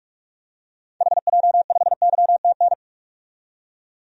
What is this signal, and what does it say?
Morse code, heard as a keyed tone, about 43 wpm, 700 Hz. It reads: HJ5YTN
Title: DXCC-100 Call Signs - 6 Characters - 45wpm